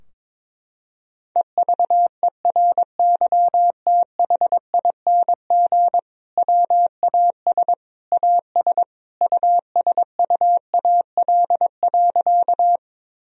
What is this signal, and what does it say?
Morse code, heard as a keyed tone, about 22 wpm, 700 Hz. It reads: EVERYTHING WAS AS USUAL.
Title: Sentences from Top 1000 Words - Review Part 2 - 22wpm